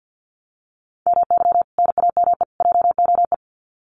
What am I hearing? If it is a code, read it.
Morse code, heard as a keyed tone, about 50 wpm, 700 Hz. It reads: MY NAME JOE
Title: Common QSO Phrases 50wpm